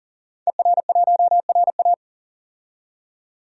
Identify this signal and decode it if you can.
Morse code, heard as a keyed tone, about 40 wpm, 700 Hz. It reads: ER1RA
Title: DXCC-100 Call Signs - 5 Characters - 40wpm